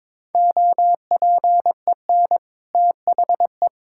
Morse code, heard as a keyed tone, about 22 wpm, 700 Hz. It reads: OPEN THE
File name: practice / 22wpm / Easy Sentences with Hints - 22wpm